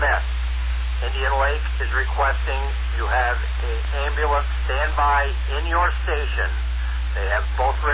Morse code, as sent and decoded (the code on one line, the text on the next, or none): none